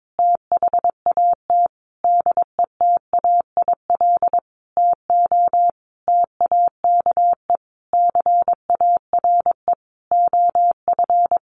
Morse code, heard as a keyed tone, about 22 wpm, 700 Hz. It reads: THAT DETAIL TO TAKE CARE OF
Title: Sentences from Top 1500 Words 22wpm